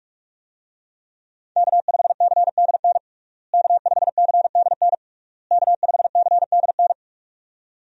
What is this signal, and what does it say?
Morse code, heard as a keyed tone, about 41 wpm, 700 Hz. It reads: K5CDN K5CDN K5CDN
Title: USA Call Signs - 5 Characters - Repeated 3x - 45wpm